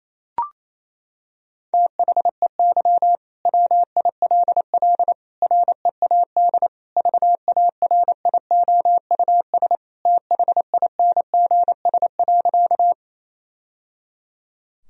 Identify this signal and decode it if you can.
Morse code, heard as a keyed tone, about 28 wpm, 700 Hz. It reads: THEY WILL READ VARIOUS THINGS.
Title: Sentences from Top 1000 Words - Review Part 4 - 28wpm